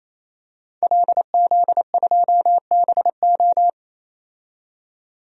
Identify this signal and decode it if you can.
Morse code, heard as a keyed tone, about 27 wpm, 700 Hz. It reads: LZ2BO
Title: DXCC-100 Call Signs - 5 Characters - 28wpm